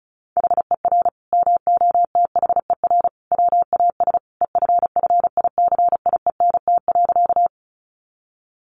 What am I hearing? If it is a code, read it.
Morse code, heard as a keyed tone, about 34 wpm, 700 Hz. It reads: HER MOTHER WAS EFFICIENT.
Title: Sentences from Top 1300 Words 35wpm